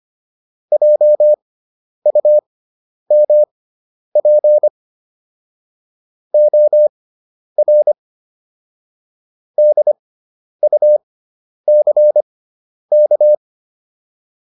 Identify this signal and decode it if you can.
Morse code, heard as a keyed tone, about 25 wpm, spaced as for 10 wpm, 600 Hz. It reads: JUMP OR DUCK